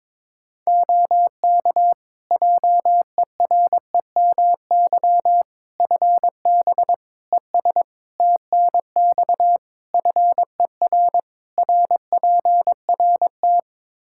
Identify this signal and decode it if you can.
Morse code, heard as a keyed tone, about 22 wpm, 700 Hz. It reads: OK JEREMY FB ES TNX FER RPRT
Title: Practice QSO - Exchange 2 - Info - 22wpm